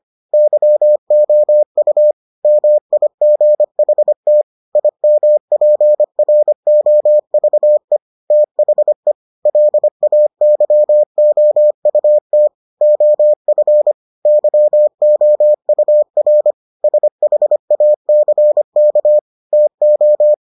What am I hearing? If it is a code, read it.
Morse code, heard as a keyed tone, about 25 wpm, 600 Hz. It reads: YOU MIGHT IMPROVE THE LAYOUT OF YOUR SHACK TO